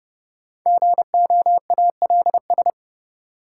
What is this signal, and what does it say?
Morse code, heard as a keyed tone, about 30 wpm, 700 Hz. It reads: GOALS
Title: Sentences from Top 1000 Words - Review Part 6 - New Word Drill - 30wpm